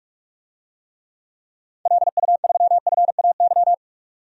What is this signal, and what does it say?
Morse code, heard as a keyed tone, about 46 wpm, 700 Hz. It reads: LU3FAY
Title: DXCC-100 Call Signs - 6 Characters - 45wpm